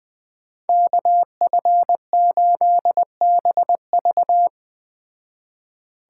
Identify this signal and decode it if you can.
Morse code, heard as a keyed tone, about 20 wpm, 700 Hz. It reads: KF8BV
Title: Speed Racing - USA 2x2 Call Signs 17wpm